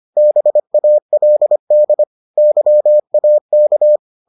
Morse code, heard as a keyed tone, about 25 wpm, 600 Hz. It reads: BALD YAK